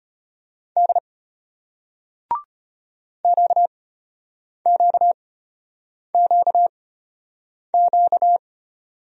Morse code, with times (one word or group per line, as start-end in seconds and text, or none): 0.8-1.0 D
3.2-8.4 QQQQ